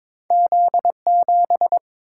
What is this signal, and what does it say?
Morse code, heard as a keyed tone, about 23 wpm, 700 Hz. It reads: Z7
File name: practice / 22wpm / Two Letter-Number Combinations - 22wpm